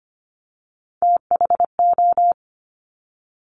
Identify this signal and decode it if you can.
Morse code, heard as a keyed tone, about 25 wpm, 700 Hz. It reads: THO
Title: Top Three Letter Combinations in Words 25wpm